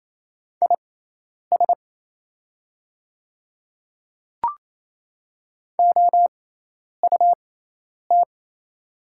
Morse code, heard as a keyed tone, about 28 wpm, spaced as for 10 wpm, 700 Hz. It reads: IS OUT
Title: 6x Character Spacing - Top 100 Words - 28wpm